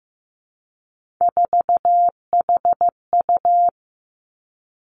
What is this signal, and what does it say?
Morse code, heard as a keyed tone, about 15 wpm, 700 Hz. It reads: 4HU
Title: Three Challenging Letter-Number Combinations 15wpm